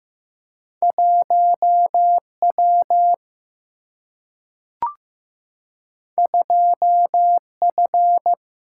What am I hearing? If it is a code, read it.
Morse code, heard as a keyed tone, about 15 wpm, 700 Hz. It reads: 1 W 2 F
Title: Single-Letter Single-Number 15wpm